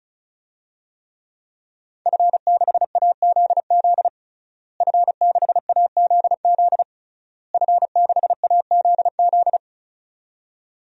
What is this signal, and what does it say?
Morse code, heard as a keyed tone, about 34 wpm, 700 Hz. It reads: F6AZZ F6AZZ F6AZZ
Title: DXCC-100 Call Signs - 5 Characters - Repeated 3x - 35wpm